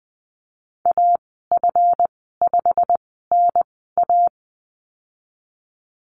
Morse code, heard as a keyed tone, about 20 wpm, spaced as for 15 wpm, 700 Hz. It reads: AF5NA
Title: 140 Lesson 23g - Letter f - Call Signs 20wpm